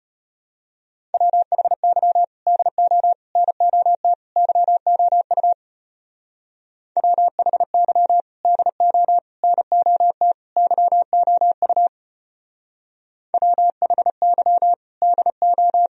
Morse code, heard as a keyed tone, about 31 wpm, 700 Hz. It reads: WHY DO NOT YOU WHY DO NOT YOU WHY DO
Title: Speed-Racing - Sets of 4 Words 25wpm